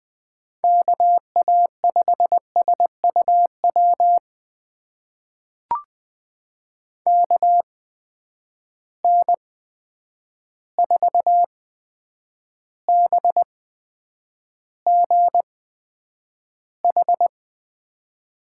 Morse code, heard as a keyed tone, about 20 wpm, spaced as for 9 wpm, 700 Hz. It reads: KA5SUW KN4BGH
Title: USA Call Signs - 6 Characters - 8x Character Spacing - 20wpm